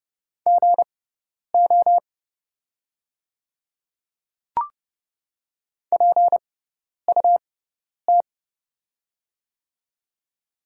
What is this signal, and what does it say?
Morse code, heard as a keyed tone, about 31 wpm, spaced as for 10 wpm, 700 Hz. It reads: GO PUT
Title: 6x Character Spacing - Top 300 Words - 30wpm